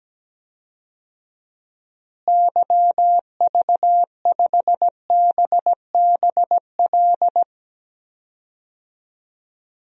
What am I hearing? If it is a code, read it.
Morse code, heard as a keyed tone, about 17 wpm, 700 Hz. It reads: YV5BBL
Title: DXCC-100 Call Signs - 6 Characters - Repeated 3x - 17wpm